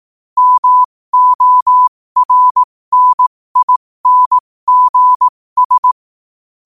Morse code, heard as a keyed tone, about 18 wpm, spaced as for 15 wpm, 1000 Hz. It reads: MORNINGS
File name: qotdfiles / 15wpm